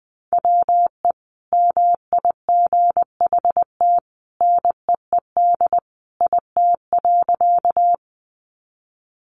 Morse code, heard as a keyed tone, about 20 wpm, 700 Hz. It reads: WE MIGHT NEED IT.
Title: A Sound Of Thunder 20wpm